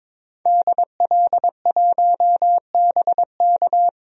Morse code, heard as a keyed tone, about 22 wpm, 700 Hz. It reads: DL1BK
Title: DXCC-100 Call Signs - 5 Characters - 22wpm